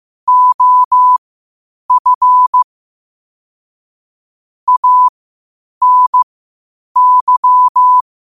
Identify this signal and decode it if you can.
Morse code, heard as a keyed tone, about 15 wpm, spaced as for 8 wpm, 1000 Hz. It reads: OF ANY
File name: qotdfiles / 8wpm